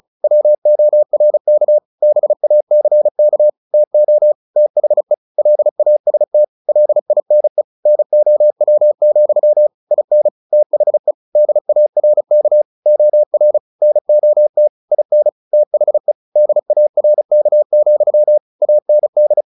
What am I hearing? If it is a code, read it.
Morse code, heard as a keyed tone, about 34 wpm, 600 Hz. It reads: WORK BACK TO THE LAST LINE NOW, IN THE DARK OR NOT IN THE DARK, AND
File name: the old man and the sea / the old man and the sea 1 35wpm